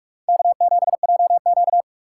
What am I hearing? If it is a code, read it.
Morse code, heard as a keyed tone, about 57 wpm, 700 Hz. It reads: K7JQ